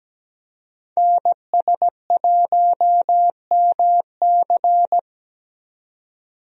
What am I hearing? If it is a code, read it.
Morse code, heard as a keyed tone, about 17 wpm, 700 Hz. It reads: NS1MC
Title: USA Call Signs - 5 Characters - 17wpm